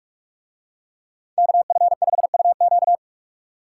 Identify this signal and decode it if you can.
Morse code, heard as a keyed tone, about 40 wpm, 700 Hz. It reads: KF5UQ